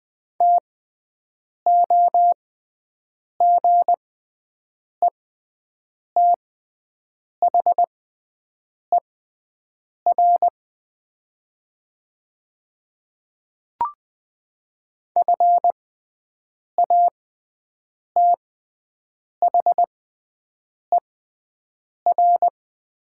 Morse code, heard as a keyed tone, about 20 wpm, spaced as for 7 wpm, 700 Hz. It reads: TOGETHER FATHER
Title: 6x Character Spacing - Top 300 Words - 20wpm